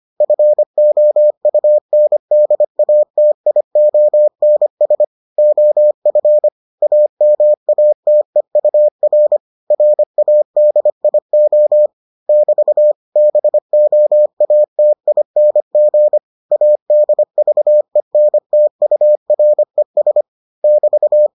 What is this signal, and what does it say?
Morse code, heard as a keyed tone, about 25 wpm, 600 Hz. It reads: FOUNDATIONS OF AMATEUR RADIO <BT> BOATING ADVENTURES <BT>